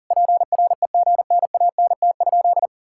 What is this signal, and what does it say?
Morse code, heard as a keyed tone, about 37 wpm, 700 Hz. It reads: PREGNANT?
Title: Sentences from Top 1000 Words - Review Part 6 - 40wpm